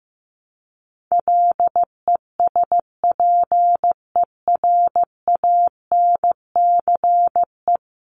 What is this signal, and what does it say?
Morse code, heard as a keyed tone, about 15 wpm, 700 Hz. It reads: LESPERANCE
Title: A Sound Of Thunder 15wpm